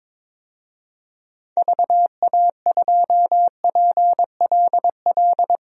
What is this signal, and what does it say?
Morse code, heard as a keyed tone, about 22 wpm, 700 Hz. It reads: VA2PLL
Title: DXCC-100 Call Signs - 6 Characters - 22wpm